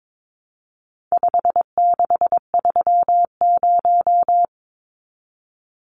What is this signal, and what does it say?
Morse code, heard as a keyed tone, about 22 wpm, 700 Hz. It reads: 5630
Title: Numbers - 4-Digit 22wpm